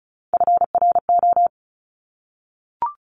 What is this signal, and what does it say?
Morse code, heard as a keyed tone, about 36 wpm, 700 Hz. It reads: FRO
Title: Top Three Letter Combinations in Words 35wpm